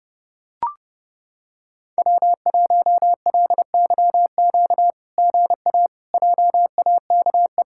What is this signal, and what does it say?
Morse code, heard as a keyed tone, about 30 wpm, 700 Hz. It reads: W1LYQ GA JAKE
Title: SOTA QSO Practice 30wpm